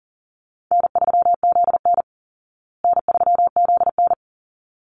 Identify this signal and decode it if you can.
Morse code, heard as a keyed tone, about 40 wpm, 700 Hz. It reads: N3ZN N3ZN
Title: Easy Call Signs Repeated 2x - 40wpm